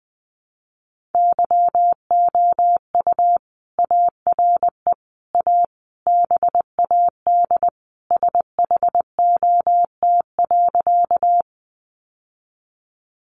Morse code, heard as a keyed tone, about 19 wpm, 700 Hz. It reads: YOU ARE A BAD SHOT.
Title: Sentences from Top 1100 Words 20wpm